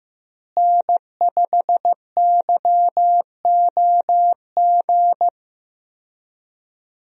Morse code, heard as a keyed tone, about 15 wpm, 700 Hz. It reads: N5YOG
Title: USA Call Signs - 5 Characters - Repeated 2x - 15wpm